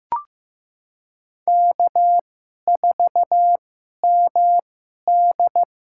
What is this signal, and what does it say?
Morse code, heard as a keyed tone, about 15 wpm, spaced as for 11 wpm, 700 Hz. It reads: K4MD
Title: USA Call Signs - 2x Character Spacing - 15wpm